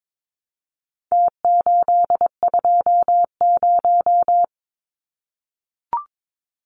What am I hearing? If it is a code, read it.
Morse code, heard as a keyed tone, about 22 wpm, 700 Hz. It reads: T820
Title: Numbers - Contest Style - 4-Digit 22wpm